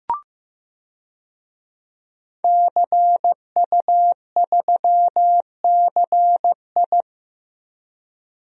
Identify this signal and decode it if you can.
Morse code, heard as a keyed tone, about 15 wpm, 700 Hz. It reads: CU3CI
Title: DXCC-100 Call Signs - 5 Characters - Repeated 2x - 15wpm